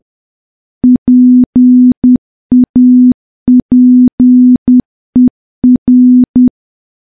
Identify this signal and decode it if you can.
Morse code, heard as a keyed tone, about 10 wpm, 250 Hz. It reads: PAPER
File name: morse / morse-training-10WPM